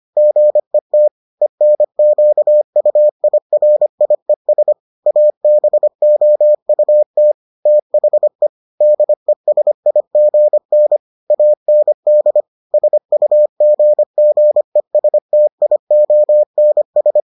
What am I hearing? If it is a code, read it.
Morse code, heard as a keyed tone, about 25 wpm, 600 Hz. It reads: GET ENQUIRIES ABOUT THE DESIGN AND SUGGESTIONS